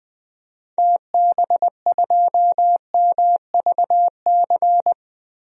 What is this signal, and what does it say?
Morse code, heard as a keyed tone, about 20 wpm, 700 Hz. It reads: TB2MVC
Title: DXCC-100 Call Signs - 6 Characters - 20wpm